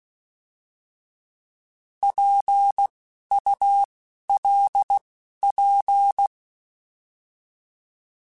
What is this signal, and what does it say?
Morse code, heard as a keyed tone, about 16 wpm, spaced as for 11 wpm, 800 Hz. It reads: PULP
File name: Alphabetical / Alphabetical mp3s / Ps @10